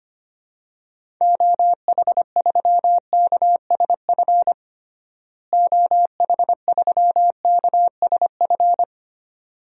Morse code, heard as a keyed tone, about 25 wpm, 700 Hz. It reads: OH3KSF OH3KSF
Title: DXCC-100 Call Signs - 6 Characters - Repeated 2x - 25wpm